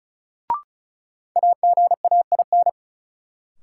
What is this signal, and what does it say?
Morse code, heard as a keyed tone, about 36 wpm, 700 Hz. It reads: AGAIN
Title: Sets of 4 Words - New Word Drill - 35wpm